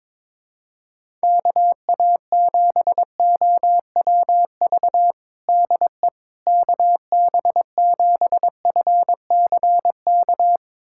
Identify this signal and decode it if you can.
Morse code, heard as a keyed tone, about 22 wpm, 700 Hz. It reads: KA7OWV DE KB7FCK
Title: USA Callsign DE Callsign - 6 Characters - 22wpm